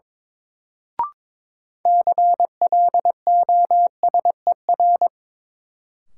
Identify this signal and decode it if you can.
Morse code, heard as a keyed tone, about 21 wpm, 700 Hz. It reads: CLOSER
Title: Sets of 5 Words Encore - New Word Drill - 22wpm